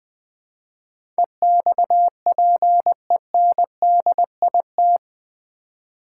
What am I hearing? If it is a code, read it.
Morse code, heard as a keyed tone, about 20 wpm, 700 Hz. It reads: EXPENDIT
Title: Build A Word - 4 Letters At A Time - 11-Letter Words - 20wpm